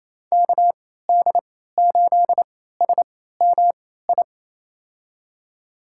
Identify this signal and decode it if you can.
Morse code, heard as a keyed tone, about 28 wpm, spaced as for 16 wpm, 700 Hz. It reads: KD8SMI
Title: USA Call Signs - 3x Character Spacing - 28wpm